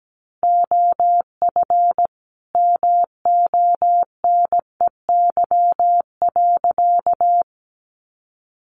Morse code, heard as a keyed tone, about 17 wpm, 700 Hz. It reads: OF MONEY.
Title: Sentences from Top 1400 Words 17wpm